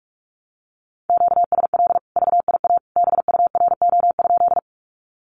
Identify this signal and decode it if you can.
Morse code, heard as a keyed tone, about 44 wpm, 700 Hz. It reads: QSL VIA BURO?
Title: Common QSO Phrases 45wpm